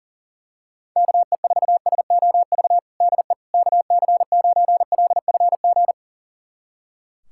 Morse code, heard as a keyed tone, about 40 wpm, 700 Hz. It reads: KE4SOV DE KC9LFG
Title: USA Callsign DE Callsign - 6 Characters - 40wpm